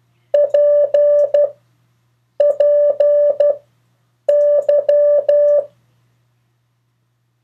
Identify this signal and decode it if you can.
Morse code, heard as a keyed tone, about 12 wpm, 580 Hz. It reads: P P Y
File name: 3of4-words-2